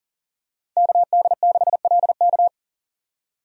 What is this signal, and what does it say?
Morse code, heard as a keyed tone, about 40 wpm, 700 Hz. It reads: KD6LK
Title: USA Call Signs - 5 Characters - 40wpm